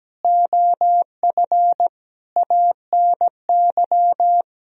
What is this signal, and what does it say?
Morse code, heard as a keyed tone, about 17 wpm, 700 Hz. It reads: OF ANY
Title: Sentences from Top 1000 Words - Review Part 4 - 17wpm